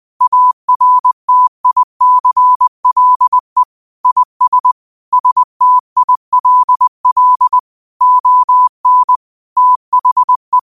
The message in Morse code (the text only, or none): ARTICLE IS STILL ON THE